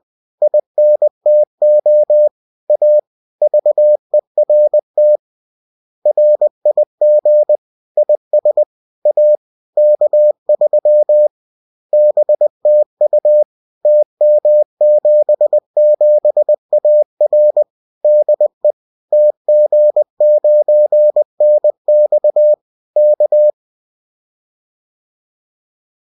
Morse code, heard as a keyed tone, about 20 wpm, 600 Hz. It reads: INTO A VERT RIG IS A K3 BTU TM77AR DE TG9NX K